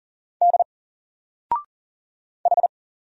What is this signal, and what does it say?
Morse code, heard as a keyed tone, about 40 wpm, 700 Hz. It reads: D H
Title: Single Letter-Number 40wpm